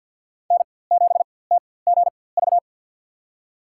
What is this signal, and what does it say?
Morse code, heard as a keyed tone, about 50 wpm, spaced as for 23 wpm, 690 Hz. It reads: N7TGV
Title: USA Call Signs - 5 Characters - 4x Character Spacing - 50wpm